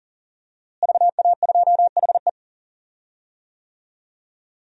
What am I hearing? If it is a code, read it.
Morse code, heard as a keyed tone, about 40 wpm, 700 Hz. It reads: VA2HE
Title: DXCC-100 Call Signs - 5 Characters - Repeated 2x - 40wpm